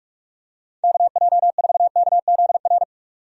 Morse code, heard as a keyed tone, about 46 wpm, 700 Hz. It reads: KJ4KZR
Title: USA Call Signs - 6 Characters - 45wpm